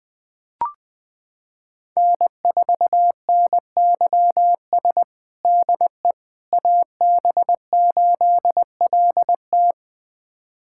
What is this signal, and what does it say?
Morse code, heard as a keyed tone, about 20 wpm, 700 Hz. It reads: N4NYS DE AB8LT
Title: USA Callsign DE Callsign - 5 Characters - 20wpm